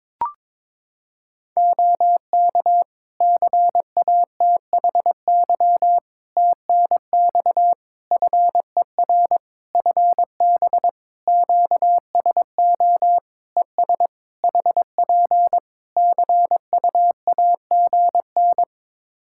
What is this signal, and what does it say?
Morse code, heard as a keyed tone, about 22 wpm, 700 Hz. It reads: OK CATHY TNX FER FB QSO ES HP CUAGN
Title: K1USN SST QSO Practice 22wpm